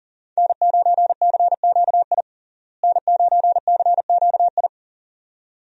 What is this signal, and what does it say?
Morse code, heard as a keyed tone, about 40 wpm, 700 Hz. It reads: N9CQI N9CQI
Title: USA Call Signs - 5 Characters - Repeated 2x - 40wpm